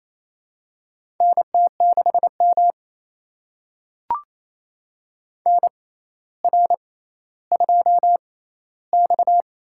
Morse code, heard as a keyed tone, about 28 wpm, spaced as for 14 wpm, 700 Hz. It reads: NT6M NR2X
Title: USA Call Signs - 4 Characters - 6x Character Spacing - 28wpm